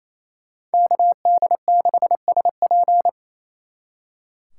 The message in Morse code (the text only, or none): KD6SP